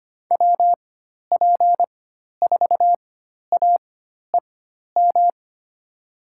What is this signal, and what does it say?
Morse code, heard as a keyed tone, about 25 wpm, spaced as for 12 wpm, 700 Hz. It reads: WP4AEM
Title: USA Call Signs - 6 Characters - 4x Character Spacing - 25wpm